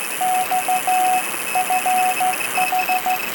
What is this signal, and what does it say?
Morse code, heard as a keyed tone, about 15 wpm, 740 Hz, with noise. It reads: XFH